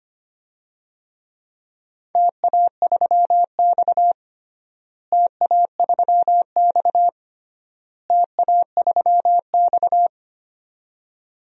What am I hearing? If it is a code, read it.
Morse code, heard as a keyed tone, about 25 wpm, 700 Hz. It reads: TA3X TA3X TA3X
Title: DXCC-100 Call Signs - 4 Characters - Repeated 3x - 25wpm